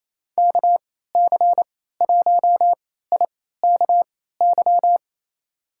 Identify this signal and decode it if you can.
Morse code, heard as a keyed tone, about 28 wpm, spaced as for 16 wpm, 700 Hz. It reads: KC1IKY
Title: USA Call Signs - 3x Character Spacing - 28wpm